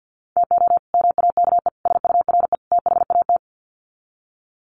Morse code, heard as a keyed tone, about 48 wpm, 700 Hz. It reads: TO MAKE SURE THAT
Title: Sets of 4 Words 50wpm